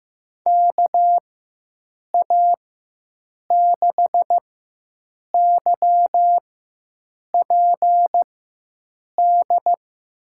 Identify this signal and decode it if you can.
Morse code, heard as a keyed tone, about 15 wpm, spaced as for 7 wpm, 700 Hz. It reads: KA6YPD
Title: USA Call Signs - 6 Characters - 4x Character Spacing - 15wpm